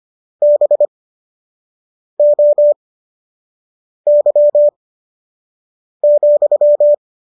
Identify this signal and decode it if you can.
Morse code, heard as a keyed tone, about 25 wpm, spaced as for 6 wpm, 590 Hz. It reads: BOY,